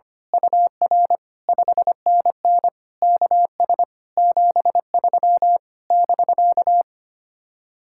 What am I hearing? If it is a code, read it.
Morse code, heard as a keyed tone, about 25 wpm, 700 Hz. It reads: UR 5NN KS 73 <BK>
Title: Speed-Racing Common QSO Phrases Encore - 25wpm